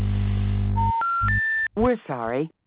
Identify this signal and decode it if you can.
Special information tone, with speech.